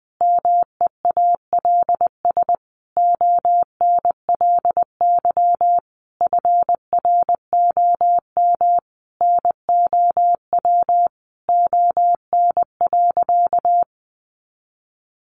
Morse code, heard as a keyed tone, about 19 wpm, 700 Hz. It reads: MEALS ONLY FROM NOW ON.
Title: Easy Phrases 20wpm